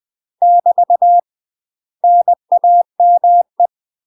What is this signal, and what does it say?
Morse code, heard as a keyed tone, about 20 wpm, 700 Hz. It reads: <BT> NAME
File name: UK MASTER QSOs TRAINING / 20WPM / UK MASTER QSOs 0040